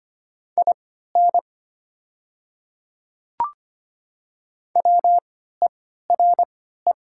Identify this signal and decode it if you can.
Morse code, heard as a keyed tone, about 25 wpm, spaced as for 14 wpm, 700 Hz. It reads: IN WERE
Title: 3x Character Spacing - Top 300 Words - 25wpm